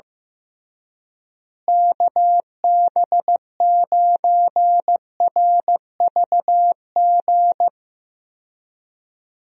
Morse code, heard as a keyed tone, about 16 wpm, 700 Hz. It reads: KB9RVG
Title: USA Call Signs - 6 Characters - Repeated 2x - 15wpm